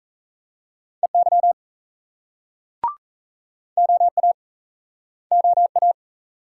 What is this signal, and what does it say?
Morse code, heard as a keyed tone, about 40 wpm, 700 Hz. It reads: EY OA OA